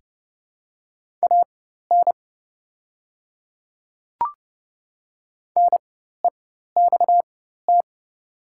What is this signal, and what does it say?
Morse code, heard as a keyed tone, about 30 wpm, spaced as for 14 wpm, 700 Hz. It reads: AN NEXT
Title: 4x Character Spacing - Top 300 Words - 30wpm